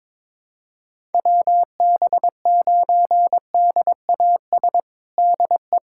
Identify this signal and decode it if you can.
Morse code, heard as a keyed tone, about 22 wpm, 700 Hz. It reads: WB9DAS DE
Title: USA Callsign DE Callsign - 6 Characters - 22wpm